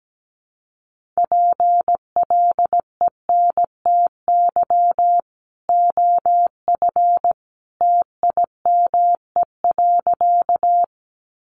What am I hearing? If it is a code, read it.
Morse code, heard as a keyed tone, about 17 wpm, 700 Hz. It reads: PLENTY OF TIME.
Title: Sentences from Top 1300 Words 17wpm